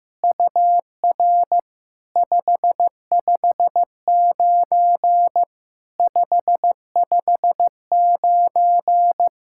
Morse code, heard as a keyed tone, about 15 wpm, 700 Hz. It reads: UR 559 559
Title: SOTA QSO Practice 15wpm